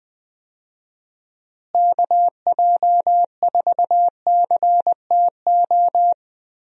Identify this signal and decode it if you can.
Morse code, heard as a keyed tone, about 20 wpm, 700 Hz. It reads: KJ4CTO
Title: USA Call Signs - 6 Characters - Repeated 2x - 20wpm